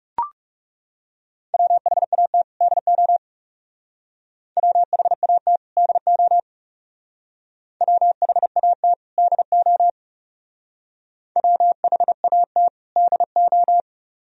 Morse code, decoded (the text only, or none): WHAT DO WHAT DO WHAT DO WHAT DO